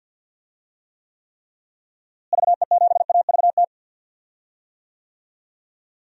Morse code, heard as a keyed tone, about 52 wpm, 700 Hz. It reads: VE7AVT
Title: DXCC-100 Call Signs - 6 Characters - Repeated 3x - 50wpm